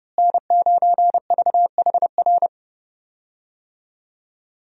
Morse code, heard as a keyed tone, about 30 wpm, 700 Hz. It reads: N9VHR
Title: USA Call Signs - 5 Characters - Repeated 3x - 30wpm